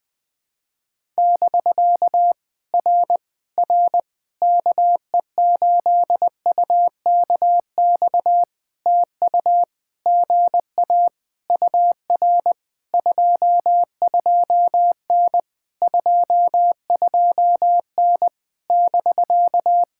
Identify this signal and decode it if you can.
Morse code, heard as a keyed tone, about 20 wpm, 700 Hz. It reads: <BK> R R KE8UKX TU GA UR 22N 22N <BK>
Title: POTA QSO Practice - 20wpm